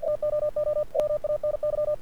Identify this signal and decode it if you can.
Morse code, heard as a keyed tone, about 47 wpm, 590 Hz. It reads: TOO MANY
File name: archived / CRSnet